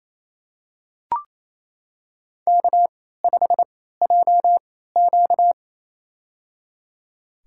Morse code, heard as a keyed tone, about 28 wpm, spaced as for 16 wpm, 700 Hz. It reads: K5JQ
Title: USA Call Signs - 4 Characters - 3x Character Spacing - 28wpm